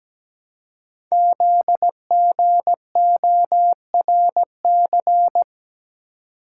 Morse code, heard as a keyed tone, about 17 wpm, 700 Hz. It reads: ZGORC
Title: Code Groups - 5 Letters - 17wpm